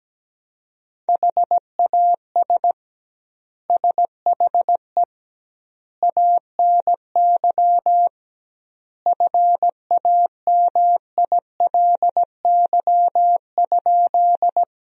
Morse code, heard as a keyed tone, about 17 wpm, 700 Hz. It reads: HAS SHE ANY FAMILY?
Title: 2x Word-Spacing - Sentences from Top 400 Words - 17wpm